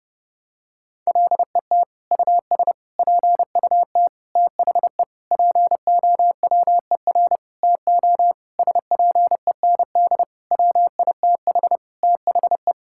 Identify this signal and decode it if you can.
Morse code, heard as a keyed tone, about 30 wpm, 700 Hz. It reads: LET US PUT THE POWER TO SPEND WITH THE
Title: Sentences from Top 1000 Words - Review Part 5 - 30wpm